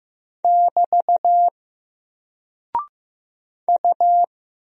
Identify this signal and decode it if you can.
Morse code, heard as a keyed tone, about 15 wpm, 700 Hz. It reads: <BT> U
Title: Single Character (Letter:Number:Punctuation:Prosign) Alt 15wpm